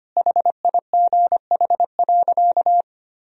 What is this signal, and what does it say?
Morse code, heard as a keyed tone, about 25 wpm, 700 Hz. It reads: HIGH.
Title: Sentences from Top 1000 Words - Review Part 1 - 25wpm